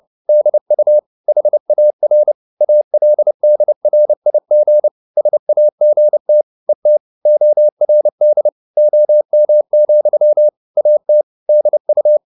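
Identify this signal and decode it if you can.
Morse code, heard as a keyed tone, about 30 wpm, 600 Hz. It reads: DU HAR ALDRIG SAGT ET ORD OM, AT DU
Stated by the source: Fast (QRQ) Morse at 29 wpm, from CW med Gnister.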